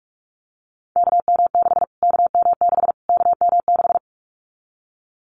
Morse code, heard as a keyed tone, about 44 wpm, 700 Hz. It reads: KM6 KM6 KM6
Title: Callsign Prefixes for Top 25 DXCC - Repeated 3x - 45wpm